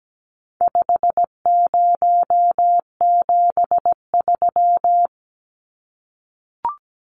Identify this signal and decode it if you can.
Morse code, heard as a keyed tone, about 17 wpm, 700 Hz. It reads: 5073
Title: Numbers - 4-Digit 17wpm